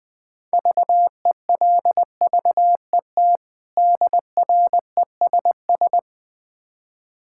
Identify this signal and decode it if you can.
Morse code, heard as a keyed tone, about 20 wpm, 700 Hz. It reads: VELVET DRESS
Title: Difficult Words with Hints - 20wpm